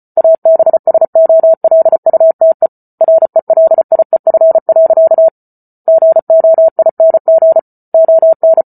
Morse code, heard as a keyed tone, about 34 wpm, 650 Hz. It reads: ABSOLUTE RELIEF. GOING ON